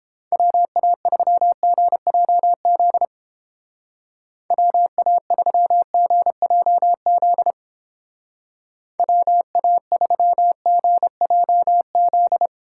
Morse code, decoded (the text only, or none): WA3GJZ WA3GJZ WA3GJZ